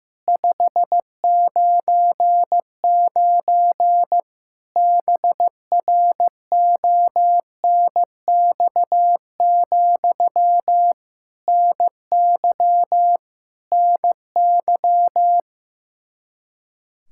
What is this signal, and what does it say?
Morse code, heard as a keyed tone, about 15 wpm, 700 Hz. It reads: 599 BRONX, NY NY
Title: Practice QSO - Exchange 1 - Beginning - 15wpm